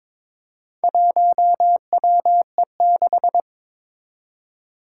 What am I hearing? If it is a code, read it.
Morse code, heard as a keyed tone, about 22 wpm, 700 Hz. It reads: 1WE6
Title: Four Challenging Letter-Number Combinations 22wpm